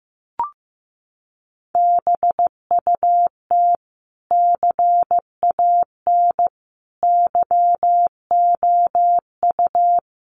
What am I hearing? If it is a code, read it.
Morse code, heard as a keyed tone, about 15 wpm, 700 Hz. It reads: BUT CAN YOU
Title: Sentences from Top 1400 Words 15wpm